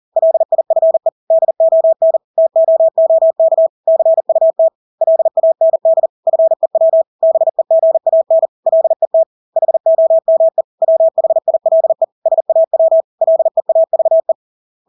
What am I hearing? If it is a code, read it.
Morse code, heard as a keyed tone, about 40 wpm, 650 Hz. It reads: LIFE DON TOOK CUT LAND FEW BEGAN LET HOME WHILE SAW LEAVE
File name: Engelske ord 40-40 wpm